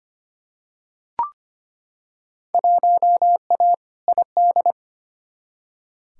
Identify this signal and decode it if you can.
Morse code, heard as a keyed tone, about 25 wpm, 700 Hz. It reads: 1A ID
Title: Field Day Exchanges Encore - 25wpm